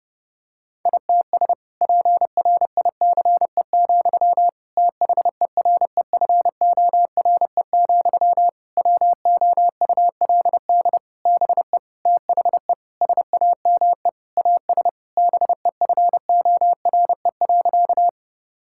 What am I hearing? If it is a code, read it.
Morse code, heard as a keyed tone, about 30 wpm, 700 Hz. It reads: ITS PRICE, THEREFORE, WOULD BE THE SAME AS BEFORE.
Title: Sentences from Top 1000 Words - Review Part 5 - 30wpm